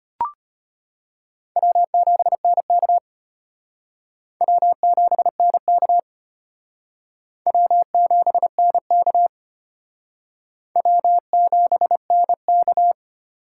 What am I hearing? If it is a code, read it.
Morse code, heard as a keyed tone, about 31 wpm, 700 Hz. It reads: W7NK W7NK W7NK W7NK